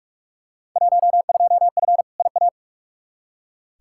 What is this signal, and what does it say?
Morse code, heard as a keyed tone, about 46 wpm, 700 Hz. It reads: 12F IA
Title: Field Day Exchanges Encore - 45wpm